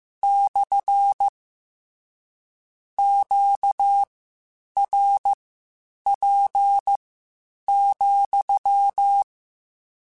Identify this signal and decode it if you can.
Morse code, heard as a keyed tone, about 15 wpm, spaced as for 8 wpm, 800 Hz. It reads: / QRP,
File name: Q Calls / Q Calls-mp3s / Q CALLS SPSL @13